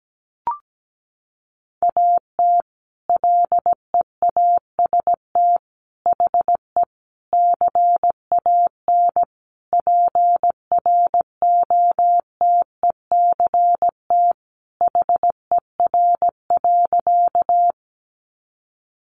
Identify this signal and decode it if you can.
Morse code, heard as a keyed tone, about 17 wpm, 700 Hz. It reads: AT LEAST HE CAN PROTECT HER.
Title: Sentences from Top 900 Words 17wpm